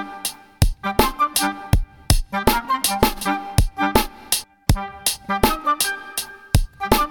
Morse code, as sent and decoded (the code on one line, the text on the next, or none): none